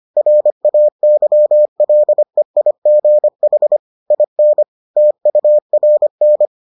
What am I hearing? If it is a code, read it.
Morse code, heard as a keyed tone, about 25 wpm, 600 Hz. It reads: RAYLEIGH IN TURN